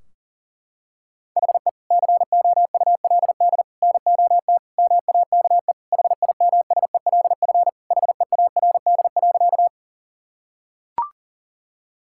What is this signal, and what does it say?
Morse code, heard as a keyed tone, about 40 wpm, 700 Hz. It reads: HE COULD NOT MAKE HIMSELF HEARD.
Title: Sentences from Top 1600 Words 40wpm